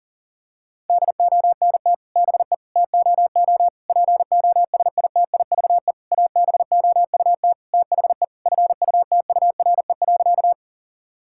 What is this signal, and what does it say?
Morse code, heard as a keyed tone, about 40 wpm, 700 Hz. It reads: DONT BE TOO POSITIVE ABOUT THE FUTURE.
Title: Sentences from Top 1000 Words - Review Part 5 - 40wpm